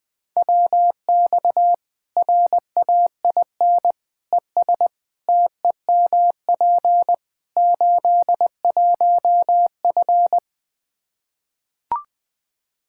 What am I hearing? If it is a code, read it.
Morse code, heard as a keyed tone, about 20 wpm, 700 Hz. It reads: WX RAIN ES TEMP 81F
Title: Practice QSO - Exchange 2 - Info - 20wpm